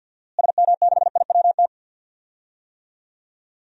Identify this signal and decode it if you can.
Morse code, heard as a keyed tone, about 53 wpm, 700 Hz. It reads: SM6IWT
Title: DXCC-100 Call Signs - 6 Characters - Repeated 2x - 50wpm